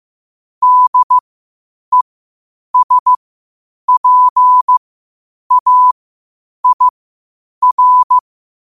Morse code, heard as a keyed tone, about 15 wpm, spaced as for 8 wpm, 1000 Hz. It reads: DESPAIR
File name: qotdfiles / 8wpm